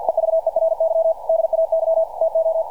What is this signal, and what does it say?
Morse code, heard as a keyed tone, about 48 wpm, 660 Hz. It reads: EVERY DAY TO
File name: archived / CRSnet